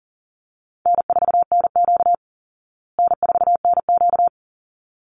Morse code, heard as a keyed tone, about 37 wpm, 700 Hz. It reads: N4NQ N4NQ
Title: Call Signs Repeated 3x - 40wpm